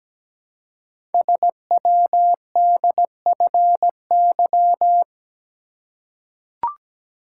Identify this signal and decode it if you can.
Morse code, heard as a keyed tone, about 17 wpm, 700 Hz. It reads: SWDFY
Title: Code Groups - 5 Letters - 17wpm